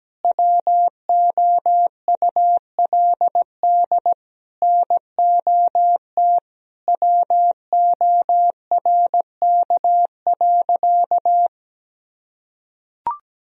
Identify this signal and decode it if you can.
Morse code, heard as a keyed tone, about 17 wpm, 700 Hz. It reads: WOULD NOT WORK.
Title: Sentences from Top 1000 Words - Review Part 3 - 17wpm